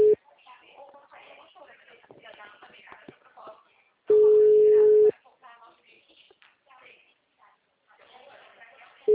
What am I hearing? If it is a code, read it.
Ringback tone (Europe, 425 Hz), with speech.
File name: out- -2020- - - .36